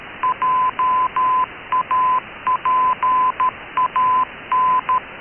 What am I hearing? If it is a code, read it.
Morse code, heard as a keyed tone, about 13 wpm, 1010 Hz. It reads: JAPAN
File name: KKS-1 cw 21jul2014 1425z